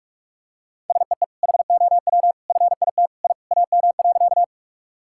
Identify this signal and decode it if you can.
Morse code, heard as a keyed tone, about 41 wpm, 700 Hz. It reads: SEE HOW FIT I AM.
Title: Sentences from Top 1000 Words - Review Part 5 - 45wpm